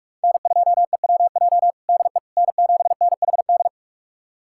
Morse code, heard as a keyed tone, about 46 wpm, 700 Hz. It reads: N2EWJ DE N7NHD
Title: USA Callsign DE Callsign - 5 Characters - 45wpm